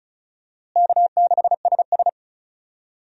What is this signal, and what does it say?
Morse code, heard as a keyed tone, about 36 wpm, 700 Hz. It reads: K6SS
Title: USA Call Signs - 4 Characters - Repeated 2x - 35wpm